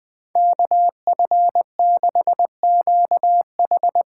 Morse code, heard as a keyed tone, about 20 wpm, 700 Hz. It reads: KF6QH
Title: USA Call Signs - 8x Character Spacing - 20wpm